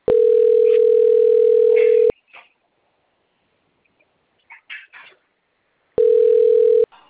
Ringback tone (North America), 440 + 480 Hz.